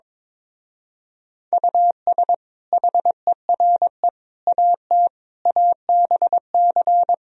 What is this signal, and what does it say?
Morse code, heard as a keyed tone, about 23 wpm, 700 Hz. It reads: US HERE AT ABC
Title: Speed-Racing - Sets of 5 Words 22wpm